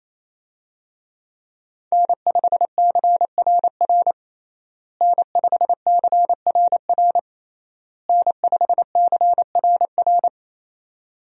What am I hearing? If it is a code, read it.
Morse code, heard as a keyed tone, about 28 wpm, 700 Hz. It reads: N5CRR N5CRR N5CRR
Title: USA Call Signs - 5 Characters - Repeated 3x - 28wpm